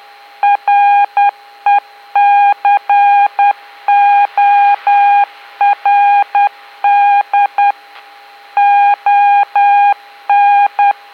Morse code, heard as a keyed tone, about 10 wpm, 810 Hz. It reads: RECORD ON